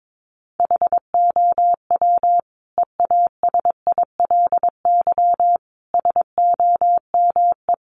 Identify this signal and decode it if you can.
Morse code, heard as a keyed tone, about 22 wpm, 700 Hz. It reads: HOW EASILY SOME
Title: Sentences from Top 900 Words 22wpm